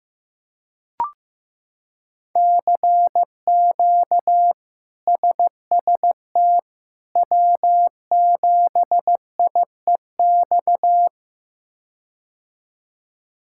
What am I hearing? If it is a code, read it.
Morse code, heard as a keyed tone, about 15 wpm, 700 Hz. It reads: CQ SST W7IEX
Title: K1USN SST QSO Practice - 15wpm